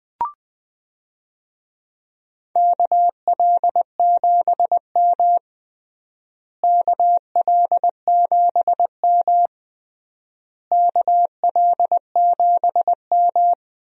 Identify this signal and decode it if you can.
Morse code, heard as a keyed tone, about 20 wpm, 700 Hz. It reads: KL7M KL7M KL7M
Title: DXCC-100 Call Signs - 4 Characters - Repeated 3x - 20wpm